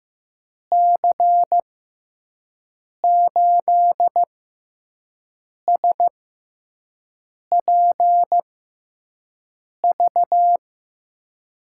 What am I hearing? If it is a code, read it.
Morse code, heard as a keyed tone, about 15 wpm, spaced as for 5 wpm, 700 Hz. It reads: C8SPV
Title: USA Call Signs - 6x Character Spacing - 15wpm